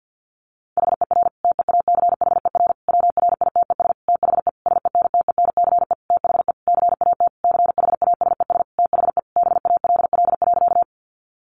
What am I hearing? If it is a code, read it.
Morse code, heard as a keyed tone, about 50 wpm, 700 Hz. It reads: HER TEACHER WRITES THE SENTENCE THE CAT CHASES THE BALL.
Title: Additional Easy Phrases 50wpm